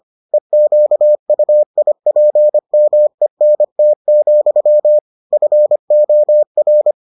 Morse code, heard as a keyed tone, about 25 wpm, 600 Hz. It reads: EQUIPMENT, FOR